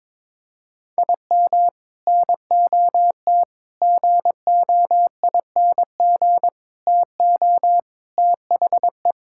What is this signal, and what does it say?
Morse code, heard as a keyed tone, about 22 wpm, 700 Hz. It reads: IM NOT GOING TO THE
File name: practice / 22wpm / Sentences with 3-Word Phrases - 22wpm